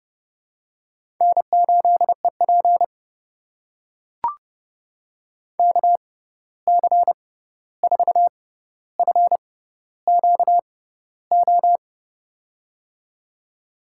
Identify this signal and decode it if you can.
Morse code, heard as a keyed tone, about 32 wpm, spaced as for 14 wpm, 700 Hz. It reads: N8EP KC4FQO
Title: USA Call Signs - 6x Character Spacing - 30wpm